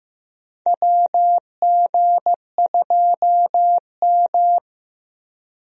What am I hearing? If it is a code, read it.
Morse code, heard as a keyed tone, about 15 wpm, 700 Hz. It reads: WG2M
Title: USA Call Signs - 4 Characters - Repeated 2x - 15wpm